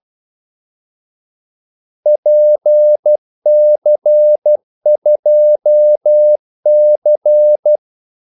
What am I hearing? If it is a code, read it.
Morse code, heard as a keyed tone, about 12 wpm, 600 Hz. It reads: PC2C
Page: 2 ekstra ordmellemrum mellem kaldesignalerne